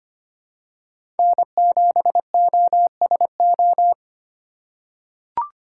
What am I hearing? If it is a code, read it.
Morse code, heard as a keyed tone, about 25 wpm, 700 Hz. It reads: N7OSO
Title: USA Call Signs - 5 Characters - 2x Character Spacing - 25wpm